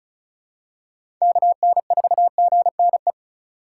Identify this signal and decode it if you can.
Morse code, heard as a keyed tone, about 34 wpm, 700 Hz. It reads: KN4GNE
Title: USA Call Signs - 6 Characters - 4x Character Spacing - 35wpm